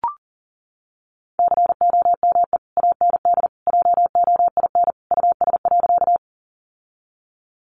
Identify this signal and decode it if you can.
Morse code, heard as a keyed tone, about 40 wpm, 700 Hz. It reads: COME AND JOIN US.
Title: Sentences from Top 1400 Words 40wpm